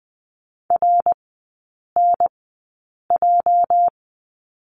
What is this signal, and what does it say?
Morse code, heard as a keyed tone, about 20 wpm, spaced as for 8 wpm, 700 Hz. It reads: RNJ
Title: 218 Lesson 35e - Number 0 - Three Characters 20wpm